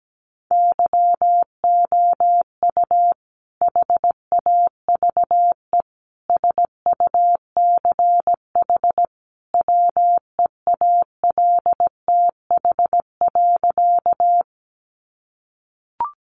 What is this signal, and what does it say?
Morse code, heard as a keyed tone, about 17 wpm, 700 Hz. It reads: YOU HAVE SUCH WEALTH.
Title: Sentences from Top 1200 Words 17wpm